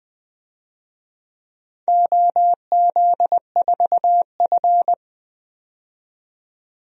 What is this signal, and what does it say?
Morse code, heard as a keyed tone, about 20 wpm, 700 Hz. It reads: OZ4F
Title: DXCC-100 Call Signs - 4 Characters - 20wpm